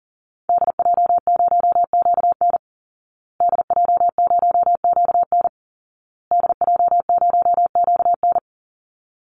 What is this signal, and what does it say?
Morse code, heard as a keyed tone, about 40 wpm, 700 Hz. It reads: DJ0QN DJ0QN DJ0QN
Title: Call Signs Repeated 2x - 40wpm